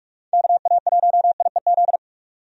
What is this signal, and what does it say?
Morse code, heard as a keyed tone, about 44 wpm, 700 Hz. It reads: KA1IEZ
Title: USA Call Signs - 6 Characters - 45wpm